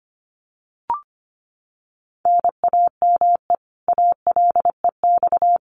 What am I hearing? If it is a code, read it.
Morse code, heard as a keyed tone, about 25 wpm, 700 Hz. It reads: NAME ALEX
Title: Common QSO Phrases 25wpm